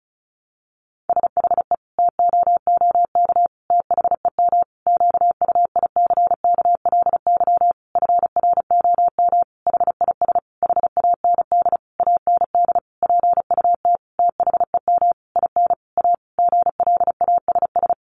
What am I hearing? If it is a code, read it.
Morse code, heard as a keyed tone, about 34 wpm, 700 Hz. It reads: SHE TOOK THEM QUICKLY FROM HIS HAND AND PUT THEM IN A GLASS
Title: Sentences from Top 1300 Words 35wpm